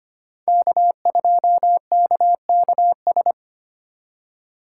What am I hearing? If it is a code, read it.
Morse code, heard as a keyed tone, about 32 wpm, 700 Hz. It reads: K2KKS